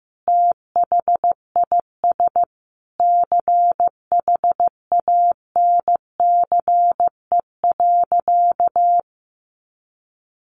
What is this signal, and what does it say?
Morse code, heard as a keyed tone, about 15 wpm, 700 Hz. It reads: THIS CHANCE.
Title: Sentences from Top 800 Words 15wpm